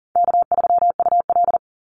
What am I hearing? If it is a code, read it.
Morse code, heard as a keyed tone, about 40 wpm, 700 Hz. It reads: K3UL
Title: Call Signs - 40wpm